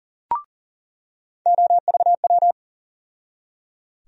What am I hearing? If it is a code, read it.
Morse code, heard as a keyed tone, about 40 wpm, 700 Hz. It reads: OVW
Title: USA Call Sign Postfixes - 3 Letters 40wpm